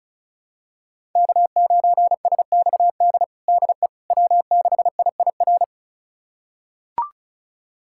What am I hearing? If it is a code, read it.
Morse code, heard as a keyed tone, about 34 wpm, 700 Hz. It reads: K9SXD DE W6IIR
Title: USA Callsign DE Callsign - 5 Characters - 35wpm